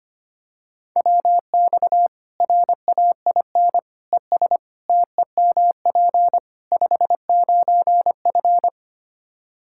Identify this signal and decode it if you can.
Morse code, heard as a keyed tone, about 25 wpm, 700 Hz. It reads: WX RAIN ES TEMP 59F
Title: Practice QSO - Exchange 2 - Info - 25wpm